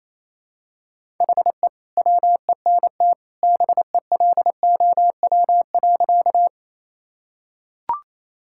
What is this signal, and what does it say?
Morse code, heard as a keyed tone, about 28 wpm, 700 Hz. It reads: HE WENT BELOW.
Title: Sentences from Top 1000 Words - Review Part 5 - 28wpm